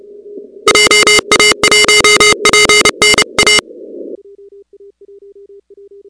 Morse code, heard as a keyed tone, about 30 wpm, 2100 Hz, with noise. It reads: JA1PNA
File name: contests / 2019-cw-wpx-kt5j-k5pi- -2330 0-13